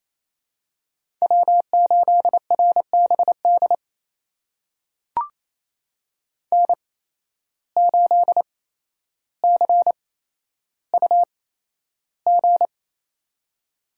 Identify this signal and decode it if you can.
Morse code, heard as a keyed tone, about 27 wpm, spaced as for 12 wpm, 700 Hz. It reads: W8RBD N8CUG
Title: USA Call Signs - 5 Characters - 8x Character Spacing - 28wpm